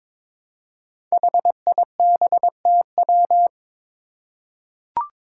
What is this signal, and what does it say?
Morse code, heard as a keyed tone, about 22 wpm, 700 Hz. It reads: HIBTW
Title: Five Challenging Letter-Number Combinations 22wpm